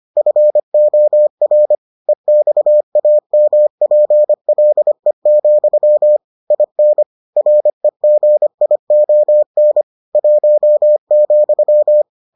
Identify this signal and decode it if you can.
Morse code, heard as a keyed tone, about 25 wpm, 600 Hz. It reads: FOR EXAMPLE, IN REGION 1,